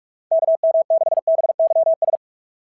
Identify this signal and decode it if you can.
Morse code, heard as a keyed tone, about 46 wpm, 690 Hz. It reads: KM6BYS
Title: SOTA QSO Practice 45wpm